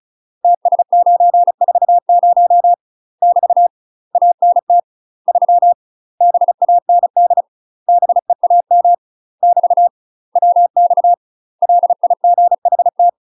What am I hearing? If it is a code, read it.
Morse code, heard as a keyed tone, about 37 wpm, 700 Hz. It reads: TS940 <BT> ANT 3 BAND BEAM <BT> WX LIGHT
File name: UK MASTER QSOs TRAINING / 35WPM / UK MASTER QSOs 0040